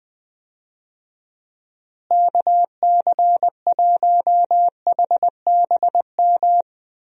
Morse code, heard as a keyed tone, about 20 wpm, 700 Hz. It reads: KC1HBM
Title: USA Call Signs - 6 Characters - Repeated 3x - 20wpm